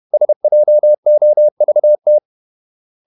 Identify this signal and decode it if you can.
Morse code, heard as a keyed tone, about 31 wpm, 600 Hz. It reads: SJOVT
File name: Ferie QRQ 31wpm